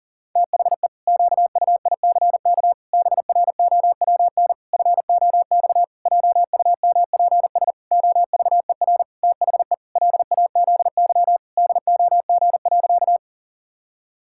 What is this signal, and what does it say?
Morse code, heard as a keyed tone, about 40 wpm, 700 Hz. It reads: THE QUICK BROWN FOX JUMPS OVER THE LAZY DOG.
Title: Pangram Challenge v2 - 40wpm